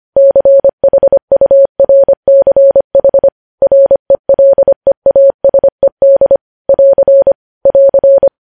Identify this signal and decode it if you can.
Morse code, heard as a keyed tone, about 25 wpm, 550 Hz. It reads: CHURCH RELEASED <AR> <AR>
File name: podcast / News headlines-1x-25x25